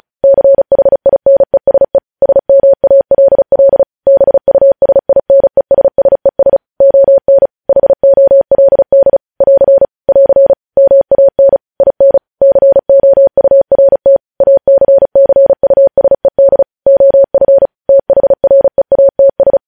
Morse code, heard as a keyed tone, about 36 wpm, 550 Hz. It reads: CHINESE SMALL BUSINESSES ON HOLD <AR> <AR> MAN IN COURT ACCUSED OF THREATS
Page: news headlines in morse code at 35 wpm